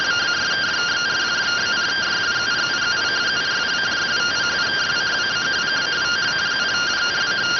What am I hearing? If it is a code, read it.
Two-tone teleprinter signal, about 170 Hz shift.